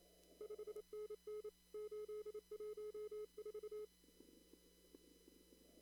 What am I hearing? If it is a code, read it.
Morse code, heard as a keyed tone, about 27 wpm, 420 Hz. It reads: 5NN 814
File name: CW / kt5j wpx cw-op-k5pi